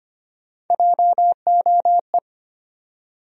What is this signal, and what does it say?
Morse code, heard as a keyed tone, about 25 wpm, 700 Hz. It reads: JOE
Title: USA Common Names - Max 5 Letters - 25wpm